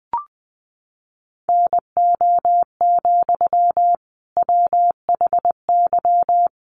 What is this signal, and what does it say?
Morse code, heard as a keyed tone, about 20 wpm, 700 Hz. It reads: NO, WHY
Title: Sentences from Top 300 Words 20wpm